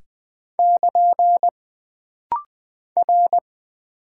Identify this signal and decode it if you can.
Morse code, heard as a keyed tone, about 20 wpm, 700 Hz. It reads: <KN> R
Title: Single Character (Letter:Number:Punctuation:Prosign) - Rapid-Fire - 20wpm